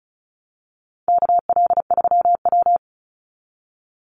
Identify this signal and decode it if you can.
Morse code, heard as a keyed tone, about 34 wpm, 700 Hz. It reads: KL3W
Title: Call Signs Repeated 3x - 35wpm